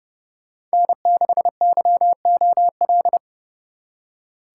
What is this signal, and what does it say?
Morse code, heard as a keyed tone, about 35 wpm, 700 Hz. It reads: N6YOL
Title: Speed Racing - USA 1x3 Call Signs 30wpm